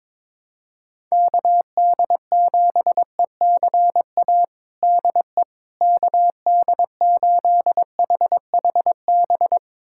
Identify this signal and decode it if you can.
Morse code, heard as a keyed tone, about 22 wpm, 700 Hz. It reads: KD7ECA DE KD8HHB
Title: USA Callsign DE Callsign - 6 Characters - 22wpm